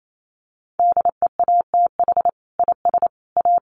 Morse code, heard as a keyed tone, about 28 wpm, 700 Hz. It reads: DEATH IS A
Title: Sentences from Top 1500 Words 28wpm